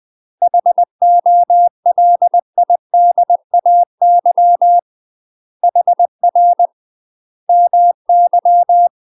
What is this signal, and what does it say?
Morse code, heard as a keyed tone, about 20 wpm, 700 Hz. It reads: HOLIDAY HR MY
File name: UK MASTER QSOs TRAINING / 20WPM / UK MASTER QSOs 0070